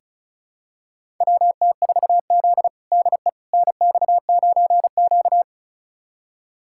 Morse code, heard as a keyed tone, about 36 wpm, 700 Hz. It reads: WT4Z DE NX9Q
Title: USA Callsign DE Callsign - 4 Characters - 35wpm